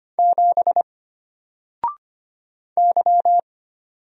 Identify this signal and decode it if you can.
Morse code, heard as a keyed tone, about 25 wpm, 700 Hz. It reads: 7 Y
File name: Single-Letter-Number-25wpm